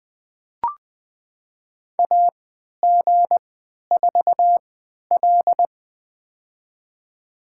Morse code, heard as a keyed tone, about 20 wpm, spaced as for 11 wpm, 700 Hz. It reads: AG4L
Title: USA Call Signs - 3x Character Spacing - 20wpm